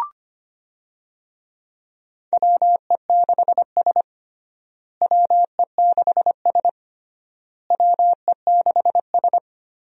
Morse code, heard as a keyed tone, about 25 wpm, 700 Hz. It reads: WE6S WE6S WE6S
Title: USA Call Signs - 4 Characters - Repeated 3x - 25wpm